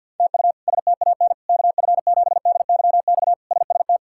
Morse code, heard as a keyed tone, about 46 wpm, 700 Hz. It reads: TU STAN KF7DYX SST
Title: K1USN SST QSO Practice - 50wpm